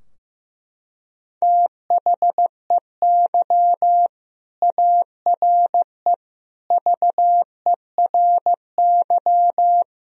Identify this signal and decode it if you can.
Morse code, heard as a keyed tone, about 16 wpm, 700 Hz. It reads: THEY ARE VERY
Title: Sentences from Top 1000 Words - Review Part 3 - 15wpm